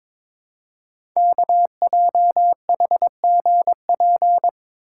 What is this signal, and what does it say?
Morse code, heard as a keyed tone, about 23 wpm, 700 Hz. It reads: KJHGP
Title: Code Groups - 5 Letters - 22wpm